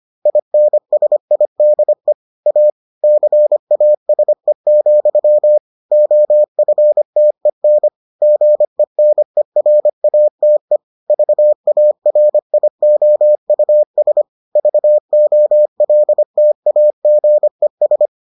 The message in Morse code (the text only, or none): INSIDE A CASE, OFTEN GENERATE VARIOUS VOLTAGES